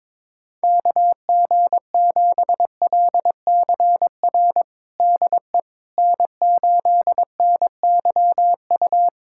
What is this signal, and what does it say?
Morse code, heard as a keyed tone, about 22 wpm, 700 Hz. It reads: KG7LCR DE N8NYU
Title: USA Callsign DE Callsign 22wpm